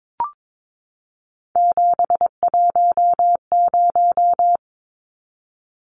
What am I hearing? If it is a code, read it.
Morse code, heard as a keyed tone, about 22 wpm, 700 Hz. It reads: 710
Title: Numbers - 3-Digit 22wpm